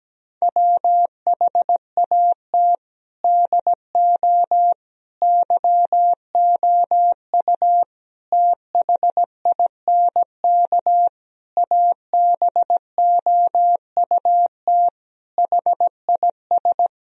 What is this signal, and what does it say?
Morse code, heard as a keyed tone, about 17 wpm, 700 Hz. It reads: WHAT DO YOU THINK ABOUT HIS
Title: Sentences from Top 1000 Words - Review Part 4 - 17wpm